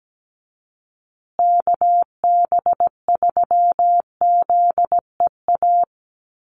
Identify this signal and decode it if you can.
Morse code, heard as a keyed tone, about 17 wpm, 700 Hz. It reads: KB3ZEA
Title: USA Call Signs Repeated 3x 17wpm